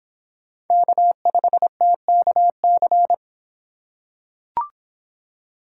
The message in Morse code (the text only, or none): K5TKC